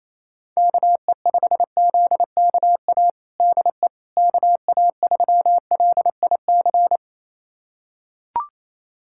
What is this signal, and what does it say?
Morse code, heard as a keyed tone, about 28 wpm, 700 Hz. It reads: KE5ZKA DE KA3LIC
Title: USA Callsign DE Callsign 28wpm